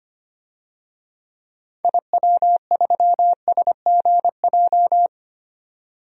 Morse code, heard as a keyed tone, about 25 wpm, 700 Hz. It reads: IW3SGJ
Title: DXCC-100 Call Signs - 6 Characters - 25wpm